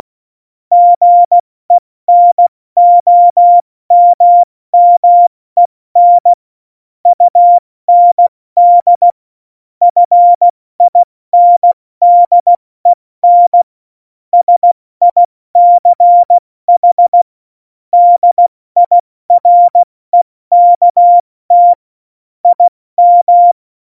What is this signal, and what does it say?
Morse code, heard as a keyed tone, about 16 wpm, spaced as for 14 wpm, 700 Hz. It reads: GENOMMEN UND FINDEN SICH DIREKT IM
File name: Morsekurs / Witze / Witz 12. Januar 2021, 70 F80